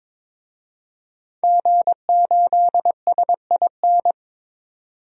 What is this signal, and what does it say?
Morse code, heard as a keyed tone, about 22 wpm, 700 Hz. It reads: G8SIN
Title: DXCC-100 Call Signs - 5 Characters - 22wpm